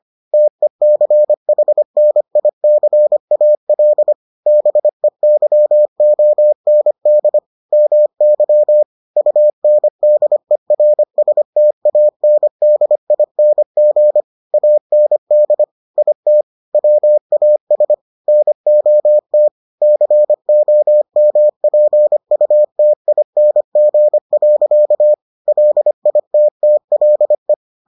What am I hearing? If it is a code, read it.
Morse code, heard as a keyed tone, about 25 wpm, 600 Hz. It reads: TECHNICAL BEYOND MY UNDERSTANDING AND IT WAS NOT COMPUTING. LITTLE